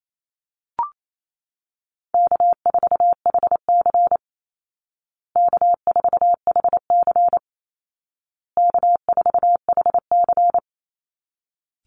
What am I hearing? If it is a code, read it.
Morse code, heard as a keyed tone, about 28 wpm, 700 Hz. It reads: K4HC K4HC K4HC
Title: Easy Call Signs Repeated 3x - 28wpm